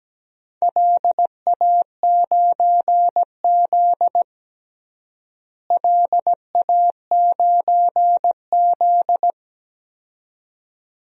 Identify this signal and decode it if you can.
Morse code, heard as a keyed tone, about 17 wpm, 700 Hz. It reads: LA9Z LA9Z
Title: DXCC-100 Call Signs - 4 Characters - Repeated 2x - 17wpm